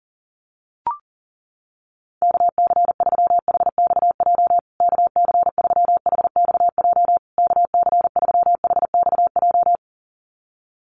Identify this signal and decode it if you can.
Morse code, heard as a keyed tone, about 40 wpm, 700 Hz. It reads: KC3HXJ KC3HXJ KC3HXJ
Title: USA Call Signs Repeated 3x 40wpm